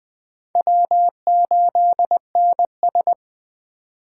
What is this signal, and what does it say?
Morse code, heard as a keyed tone, about 19 wpm, 700 Hz. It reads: W8NS